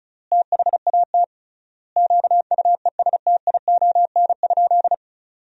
Morse code, heard as a keyed tone, about 34 wpm, 700 Hz. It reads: THAT QUESTION?
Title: 3x Word-Spacing - Sentences from Top 400 Words - 35wpm